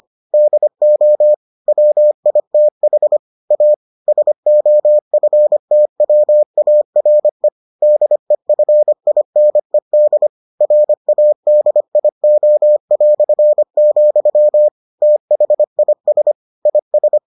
Morse code, heard as a keyed tone, about 25 wpm, 600 Hz. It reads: DO WITH A SOFTWARE DEFINED RADIO", THIS IS